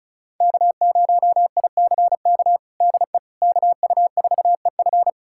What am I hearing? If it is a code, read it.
Morse code, heard as a keyed tone, about 36 wpm, 700 Hz. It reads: K0ICK DE KU4EF
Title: USA Callsign DE Callsign - 5 Characters - 35wpm